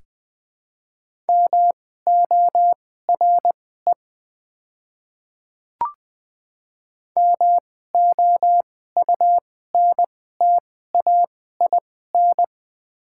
Morse code, heard as a keyed tone, about 20 wpm, spaced as for 14 wpm, 700 Hz. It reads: MORE MOUNTAIN
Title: 2x Character Spacing - Top 300 Words - 20wpm